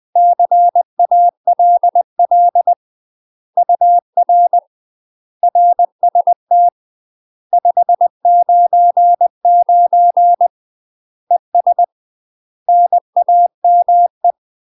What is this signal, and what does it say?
Morse code, heard as a keyed tone, about 20 wpm, 700 Hz. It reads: CALL UR RST 599 ES NAME
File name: UK MASTER QSOs TRAINING / 20WPM / UK MASTER QSOs 0030